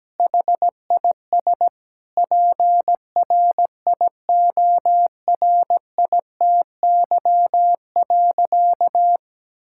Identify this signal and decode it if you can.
Morse code, heard as a keyed tone, about 17 wpm, 700 Hz. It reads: HIS PRIORITY.
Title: Sentences from Top 1600 Words 17wpm